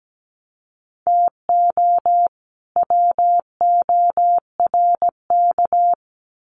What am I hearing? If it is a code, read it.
Morse code, heard as a keyed tone, about 17 wpm, 700 Hz. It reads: TO WORK
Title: Sets of 2 Words 17wpm